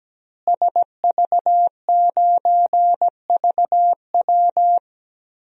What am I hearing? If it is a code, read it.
Morse code, heard as a keyed tone, about 17 wpm, 700 Hz. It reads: SV9VW
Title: DXCC-100 Call Signs - 5 Characters - Repeated 3x - 17wpm